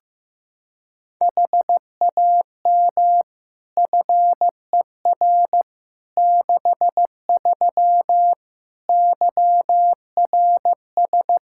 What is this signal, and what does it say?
Morse code, heard as a keyed tone, about 15 wpm, 700 Hz. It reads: HAM FER 63 YRS
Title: Practice QSO - Exchange 3 - More Info - 15wpm